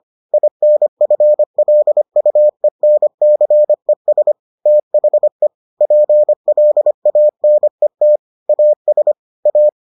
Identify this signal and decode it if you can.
Morse code, heard as a keyed tone, about 25 wpm, 600 Hz. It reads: INFLUENCES THE PLANET AS A